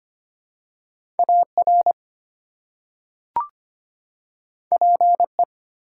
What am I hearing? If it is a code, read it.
Morse code, heard as a keyed tone, about 25 wpm, 700 Hz. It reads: AR PE
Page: morse-code-ninja-practice-25wpm